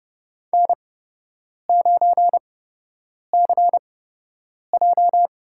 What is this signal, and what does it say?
Morse code, heard as a keyed tone, about 30 wpm, spaced as for 8 wpm, 700 Hz. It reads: N9CJ
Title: USA Call Signs - 5 Characters - 8x Character Spacing - 30wpm